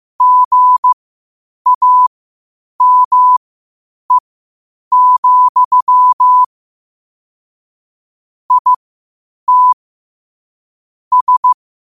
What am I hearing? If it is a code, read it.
Morse code, heard as a keyed tone, about 15 wpm, spaced as for 8 wpm, 1000 Hz. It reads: GAME, IT S
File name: qotdfiles / 8wpm